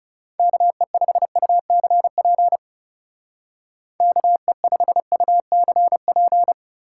Morse code, heard as a keyed tone, about 38 wpm, 700 Hz. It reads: KE5UCP KE5UCP